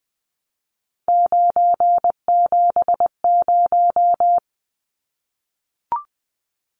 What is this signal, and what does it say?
Morse code, heard as a keyed tone, about 20 wpm, 700 Hz. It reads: 970
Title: Numbers - 3-Digit 20wpm